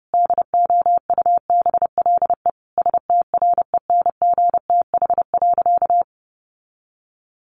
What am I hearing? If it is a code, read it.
Morse code, heard as a keyed tone, about 30 wpm, 700 Hz. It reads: DOUBLE STRENGTH.
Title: Sentences from Top 1400 Words 30wpm